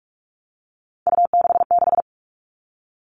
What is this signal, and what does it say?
Morse code, heard as a keyed tone, about 44 wpm, 700 Hz. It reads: U66
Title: All Cut Numbers - 3-Digit 45wpm